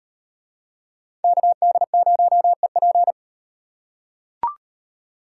Morse code, heard as a keyed tone, about 38 wpm, 700 Hz. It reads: KD0EP